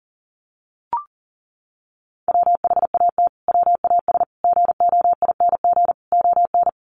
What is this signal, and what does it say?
Morse code, heard as a keyed tone, about 40 wpm, 700 Hz. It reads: WHAT WAS GOING ON
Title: Sets of 4 Words 40wpm